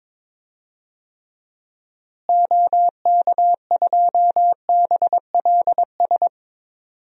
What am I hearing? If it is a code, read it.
Morse code, heard as a keyed tone, about 22 wpm, 700 Hz. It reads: OK2BLS
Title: DXCC-100 Call Signs - 6 Characters - Repeated 2x - 22wpm